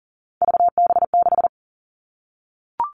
Morse code, heard as a keyed tone, about 40 wpm, 700 Hz. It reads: VB6
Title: All Cut Numbers - 3-Digit 40wpm